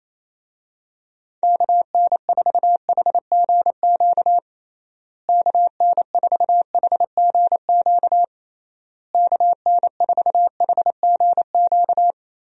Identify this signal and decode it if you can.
Morse code, heard as a keyed tone, about 28 wpm, 700 Hz. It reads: KN4HGQ KN4HGQ KN4HGQ
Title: USA Call Signs - 6 Characters - Repeated 3x - 28wpm